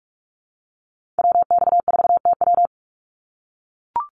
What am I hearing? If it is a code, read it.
Morse code, heard as a keyed tone, about 44 wpm, 700 Hz. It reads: WX4TW
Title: NA SOTA Activator Callsigns 45wpm